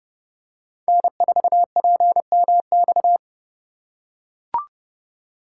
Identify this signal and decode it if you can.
Morse code, heard as a keyed tone, about 30 wpm, 700 Hz. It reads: N4PMX